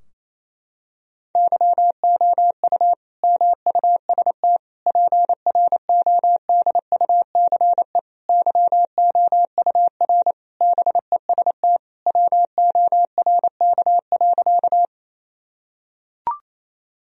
Morse code, heard as a keyed tone, about 28 wpm, 700 Hz. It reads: YOU MUST PRODUCE YOUR BEST WORK.
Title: Sentences from Top 1000 Words - Review Part 3 - 28wpm